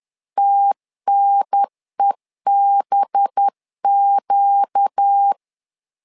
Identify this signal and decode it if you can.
Morse code, heard as a keyed tone, about 10 wpm, 800 Hz. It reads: TNEBQ
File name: LEZIONI / 5 LBVC / 5 LBVC 10wpm 2349